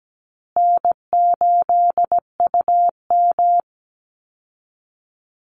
Morse code, heard as a keyed tone, about 17 wpm, 700 Hz. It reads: N8UM
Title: Call Signs Repeated 2x - 17wpm